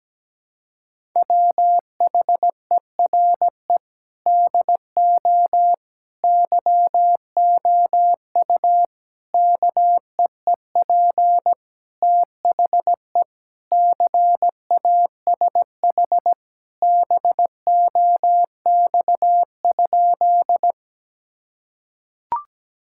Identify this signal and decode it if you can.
Morse code, heard as a keyed tone, about 17 wpm, 700 Hz. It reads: WHERE DO YOU KEEP THE CASH BOX?
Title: Sentences from Top 1000 Words - Review Part 5 - 17wpm